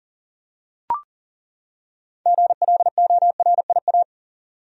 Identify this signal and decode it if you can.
Morse code, heard as a keyed tone, about 38 wpm, 700 Hz. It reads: GLORIA
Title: USA Common Names - 40wpm